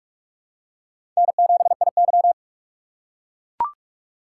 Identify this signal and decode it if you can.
Morse code, heard as a keyed tone, about 44 wpm, 700 Hz. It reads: N7IY
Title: USA Call Signs - 4 Characters - 6x Character Spacing - 45wpm